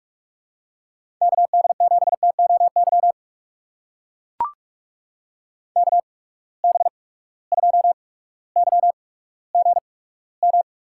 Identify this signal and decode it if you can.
Morse code, heard as a keyed tone, about 44 wpm, spaced as for 19 wpm, 700 Hz. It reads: KD7TOY KB2YGM
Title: USA Call Signs - 6 Characters - 8x Character Spacing - 45wpm